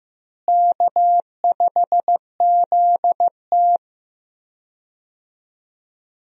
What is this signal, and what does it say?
Morse code, heard as a keyed tone, about 15 wpm, 700 Hz. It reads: K5ZT
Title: USA Call Signs - 4 Characters - Repeated 2x - 15wpm